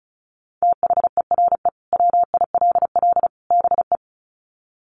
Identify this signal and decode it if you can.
Morse code, heard as a keyed tone, about 34 wpm, 700 Hz. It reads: THERE WILL BE
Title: Sets of 3 Words 35wpm